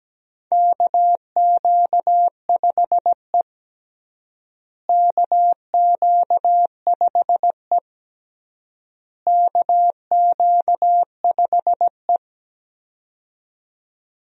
Morse code, heard as a keyed tone, about 17 wpm, 700 Hz. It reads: KQ5E KQ5E KQ5E
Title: USA Call Signs - 4 Characters - Repeated 3x - 17wpm